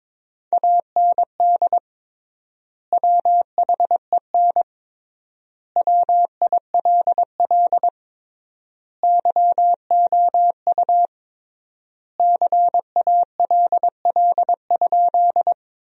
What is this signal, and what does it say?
Morse code, heard as a keyed tone, about 22 wpm, 700 Hz. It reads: AND WHEN WILL YOU CALL?
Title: 3x Word-Spacing - Sentences from Top 100 Words - 22wpm